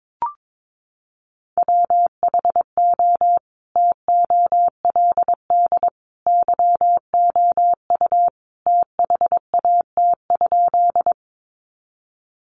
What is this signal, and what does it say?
Morse code, heard as a keyed tone, about 22 wpm, 700 Hz. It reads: WHO TOLD YOU THAT?
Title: Sentences from Top 400 Words 22wpm